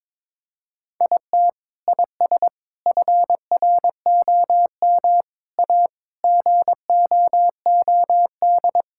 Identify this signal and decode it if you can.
Morse code, heard as a keyed tone, about 22 wpm, 700 Hz. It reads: IT IS FROM A GOOD
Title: Sentences from Top 1000 Words - Review Part 5 - 22wpm